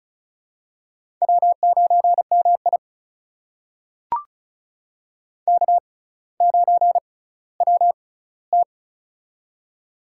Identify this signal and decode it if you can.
Morse code, heard as a keyed tone, about 34 wpm, spaced as for 18 wpm, 700 Hz. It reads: W9MI K9WT
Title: USA Call Signs - 4 Characters - 6x Character Spacing - 35wpm